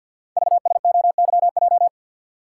Morse code, heard as a keyed tone, about 50 wpm, 700 Hz. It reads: VSOYJ
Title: Code Groups - 5 Letters - 50wpm